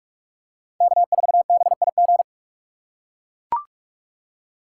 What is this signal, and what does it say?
Morse code, heard as a keyed tone, about 44 wpm, 690 Hz. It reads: K4BIG
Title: USA Call Signs - 5 Characters - 6x Character Spacing - 45wpm